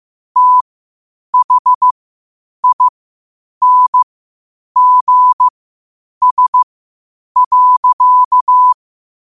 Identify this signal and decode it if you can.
Morse code, heard as a keyed tone, about 15 wpm, spaced as for 8 wpm, 1000 Hz. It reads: THINGS.
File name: qotdfiles / 8wpm